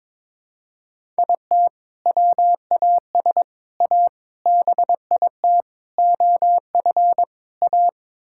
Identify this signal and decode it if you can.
Morse code, heard as a keyed tone, about 22 wpm, 700 Hz. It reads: IT WAS A BIT OF A
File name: practice / 22wpm / Sets of 6 Words - 22wpm